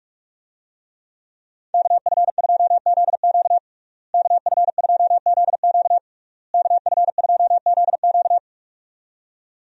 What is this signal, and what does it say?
Morse code, heard as a keyed tone, about 47 wpm, 700 Hz. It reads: KF2ZQ KF2ZQ KF2ZQ
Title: USA Call Signs - 5 Characters - Repeated 3x - 45wpm